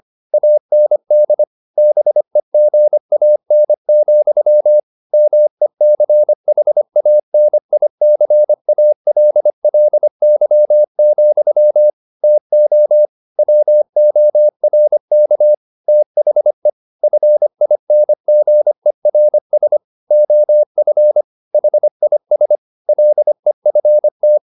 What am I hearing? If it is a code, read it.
Morse code, heard as a keyed tone, about 25 wpm, 600 Hz. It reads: AND BEGAN, MECHANICALLY, TO WORK THE FINGERS OF HIS LEFT
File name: the old man and the sea part 1 25wpm